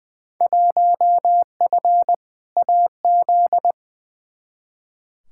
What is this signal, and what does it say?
Morse code, heard as a keyed tone, about 21 wpm, 700 Hz. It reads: 1F AZ
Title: Field Day Exchanges - 20wpm